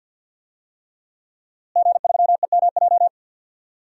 Morse code, heard as a keyed tone, about 47 wpm, 700 Hz. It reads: G3EMJ
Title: DXCC-100 Call Signs - 5 Characters - Repeated 3x - 50wpm